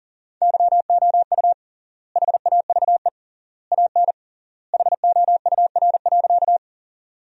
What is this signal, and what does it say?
Morse code, heard as a keyed tone, about 40 wpm, 700 Hz. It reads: YOU HAVE AN HOUR.
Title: 3x Word-Spacing - Sentences from Top 400 Words - 40wpm